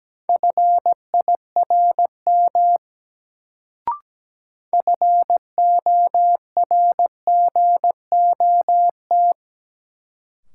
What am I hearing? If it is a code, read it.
Morse code, heard as a keyed tone, about 17 wpm, 700 Hz. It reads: FIRM FORGOT
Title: Sentences from Top 1400 Words - New Word Drill Alt - 17wpm